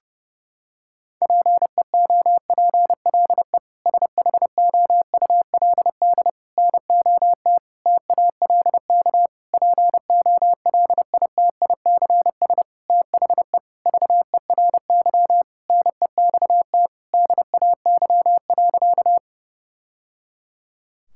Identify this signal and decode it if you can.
Morse code, heard as a keyed tone, about 30 wpm, 700 Hz. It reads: PEOPLE SHOULD NOT TALK POLITICS THE VERY NEXT DAY.
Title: Sentences from Top 1000 Words - Review Part 6 - 30wpm